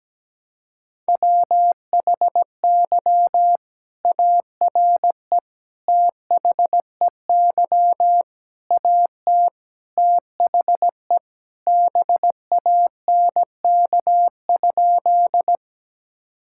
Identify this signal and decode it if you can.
Morse code, heard as a keyed tone, about 17 wpm, 700 Hz. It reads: WHY ARE THEY AT THE BANK?
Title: Sentences from Top 1000 Words - Review Part 6 - 17wpm